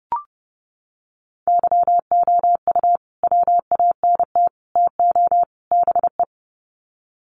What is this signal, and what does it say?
Morse code, heard as a keyed tone, about 30 wpm, 700 Hz. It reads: YOU WANT TO BE
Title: Sets of 4 Words 30wpm